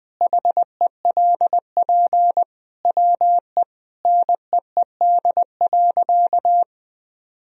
Morse code, heard as a keyed tone, about 20 wpm, 700 Hz. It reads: HELP WE NEED.
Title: Sentences from Top 1000 Words - Review Part 1 - 20wpm